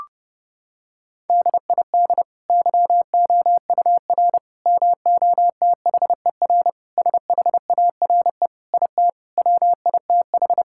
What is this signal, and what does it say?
Morse code, heard as a keyed tone, about 30 wpm, 700 Hz. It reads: DID YOUR MOTHER SHARE IT WITH
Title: Sentences from Top 1000 Words - Review Part 6 - 30wpm